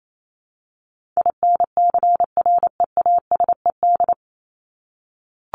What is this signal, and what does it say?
Morse code, heard as a keyed tone, about 28 wpm, 700 Hz. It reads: INCREASED
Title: ED Suffix - Top 100 Words with Base Words 28wpm